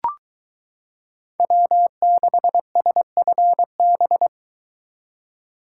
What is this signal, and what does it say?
Morse code, heard as a keyed tone, about 22 wpm, 700 Hz. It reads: W6SFB